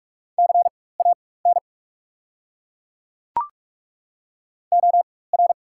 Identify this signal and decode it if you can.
Morse code, heard as a keyed tone, about 42 wpm, spaced as for 20 wpm, 690 Hz. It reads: CAN OR
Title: 4x Character Spacing - Top 200 Words - 45wpm